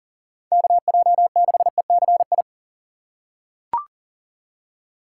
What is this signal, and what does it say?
Morse code, heard as a keyed tone, about 40 wpm, 700 Hz. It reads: KJ6ECI
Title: USA Call Signs - 6 Characters - 6x Character Spacing - 40wpm